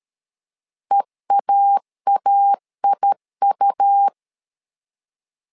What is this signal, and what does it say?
Morse code, heard as a keyed tone, about 13 wpm, 800 Hz. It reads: EAAIU
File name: LEZIONI / 3 MWRO / 3 M 12wpm 1541